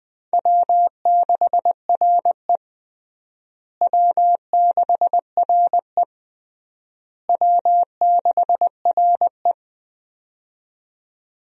Morse code, heard as a keyed tone, about 21 wpm, 700 Hz. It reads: W6RE W6RE W6RE
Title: USA Call Signs - 4 Characters - Repeated 3x - 20wpm